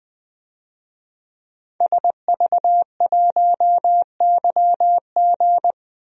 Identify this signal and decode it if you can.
Morse code, heard as a keyed tone, about 20 wpm, 700 Hz. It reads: SV1YG
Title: DXCC-100 Call Signs - 5 Characters - Repeated 2x - 20wpm